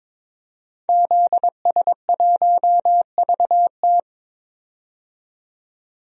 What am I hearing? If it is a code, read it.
Morse code, heard as a keyed tone, about 22 wpm, 700 Hz. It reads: ZS1VT
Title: DXCC-100 Call Signs - 5 Characters - 22wpm